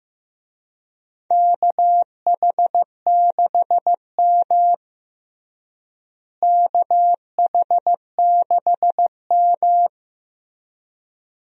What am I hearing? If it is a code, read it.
Morse code, heard as a keyed tone, about 15 wpm, 700 Hz. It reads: KH6M KH6M
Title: DXCC-100 Call Signs - 4 Characters - Repeated 2x - 15wpm